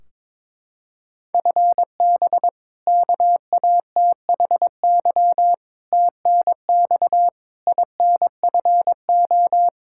Morse code, heard as a keyed tone, about 22 wpm, 700 Hz. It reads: FB KATHY TNX INFO
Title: Practice QSO - Exchange 3 - More Info - 22wpm